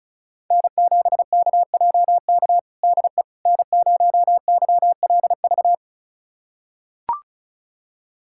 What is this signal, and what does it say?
Morse code, heard as a keyed tone, about 36 wpm, 690 Hz. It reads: N7KJK DE N0YLV
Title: USA Callsign DE Callsign - 5 Characters - 35wpm